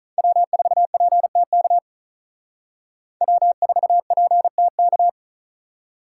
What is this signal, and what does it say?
Morse code, heard as a keyed tone, about 45 wpm, 700 Hz. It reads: W4PTK W4PTK